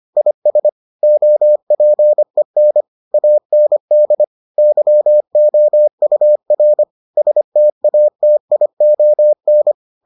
Morse code, heard as a keyed tone, about 25 wpm, 600 Hz. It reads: IS OPEN AND YOUR STATION